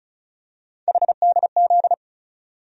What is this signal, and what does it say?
Morse code, heard as a keyed tone, about 36 wpm, 700 Hz. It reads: HDZ
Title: USA Call Sign Postfixes - 3 Letters 35wpm